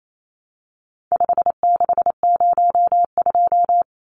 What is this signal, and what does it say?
Morse code, heard as a keyed tone, about 28 wpm, 700 Hz. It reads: 5602
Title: Numbers - 4-Digit 28wpm